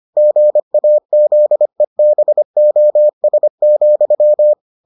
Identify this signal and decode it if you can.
Morse code, heard as a keyed tone, about 25 wpm, 600 Hz. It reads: GAZEBOS,